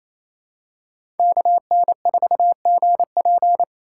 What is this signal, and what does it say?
Morse code, heard as a keyed tone, about 27 wpm, 700 Hz. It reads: KN4GP
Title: USA Call Signs - 3x Character Spacing - 28wpm